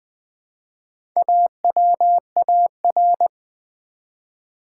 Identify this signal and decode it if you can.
Morse code, heard as a keyed tone, about 20 wpm, 700 Hz. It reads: AWAR
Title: Build A Word - 1 Letter At A Time - 5-Letter Words - 20wpm